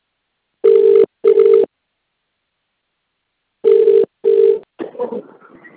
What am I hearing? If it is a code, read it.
Ringback tone (UK), 400 + 450 Hz.